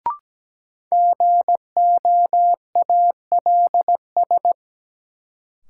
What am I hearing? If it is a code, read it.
Morse code, heard as a keyed tone, about 17 wpm, 700 Hz. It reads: GOALS
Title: Sentences from Top 1000 Words - Review Part 6 - New Word Drill - 17wpm